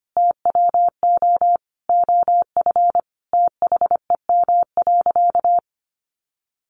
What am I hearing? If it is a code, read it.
Morse code, heard as a keyed tone, about 25 wpm, 700 Hz. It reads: TWO OF THEM.
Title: Sentences from Top 100 Words 25wpm